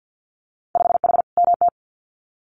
Morse code, heard as a keyed tone, about 50 wpm, 700 Hz. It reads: 5H MT
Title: Winter Field Day Exchanges Encore 50wpm